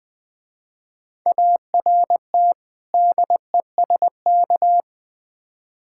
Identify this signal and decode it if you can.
Morse code, heard as a keyed tone, about 25 wpm, 700 Hz. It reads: ART DESK
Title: Difficult Words - Speed-Racing with Hints - 20wpm